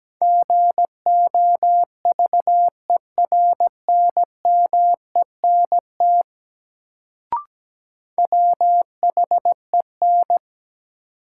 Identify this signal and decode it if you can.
Morse code, heard as a keyed tone, about 17 wpm, 700 Hz. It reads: GOVERNMENT WHEN
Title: Sets of 2 Words - New Word Drill - 17wpm